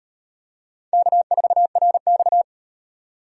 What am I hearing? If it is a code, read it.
Morse code, heard as a keyed tone, about 33 wpm, 700 Hz. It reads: K4RX